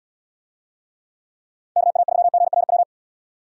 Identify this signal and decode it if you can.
Morse code, heard as a keyed tone, about 79 wpm, 700 Hz. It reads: KA3GRW